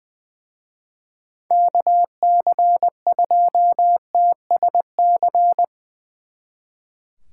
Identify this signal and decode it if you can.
Morse code, heard as a keyed tone, about 21 wpm, 700 Hz. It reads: KC2TSC
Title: Speed Racing - USA Call Signs 20wpm